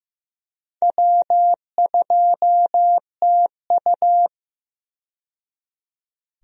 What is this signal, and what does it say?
Morse code, heard as a keyed tone, about 15 wpm, 700 Hz. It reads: W2TU
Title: USA Call Signs - 4 Characters - 15wpm